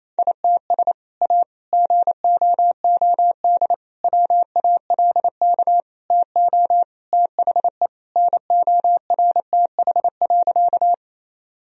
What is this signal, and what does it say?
Morse code, heard as a keyed tone, about 28 wpm, 700 Hz. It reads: ITS A GOOD WALK TO THE NORTH.
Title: Sentences from Top 1000 Words - Review Part 1 - 28wpm